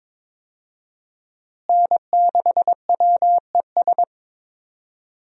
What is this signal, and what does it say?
Morse code, heard as a keyed tone, about 22 wpm, 700 Hz. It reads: N6WES
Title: USA Call Signs - 5 Characters - Repeated 3x - 22wpm